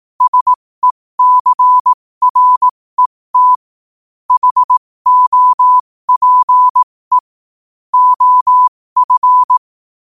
Morse code, heard as a keyed tone, about 18 wpm, spaced as for 15 wpm, 1000 Hz. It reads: SECRET HOPE OF
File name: qotdfiles / 15wpm